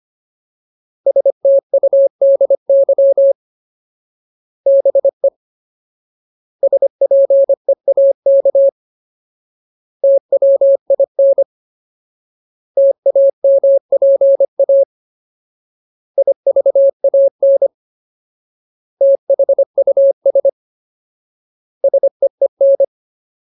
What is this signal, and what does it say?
Morse code, heard as a keyed tone, about 25 wpm, 550 Hz. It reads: STUDY BE SPEAK TWIN TAMPA IVAN THUS SEEN
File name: Ultimate Word List / 25WPM Ultimate